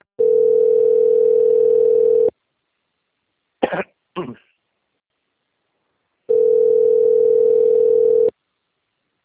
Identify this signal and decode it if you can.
Ringback tone (North America), 440 + 480 Hz.